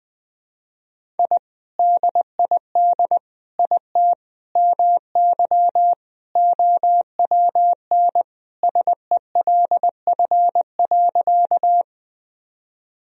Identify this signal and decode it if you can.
Morse code, heard as a keyed tone, about 20 wpm, 700 Hz. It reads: I DID IT MY OWN SELF.
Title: Sentences from Top 1000 Words - Review Part 1 - 20wpm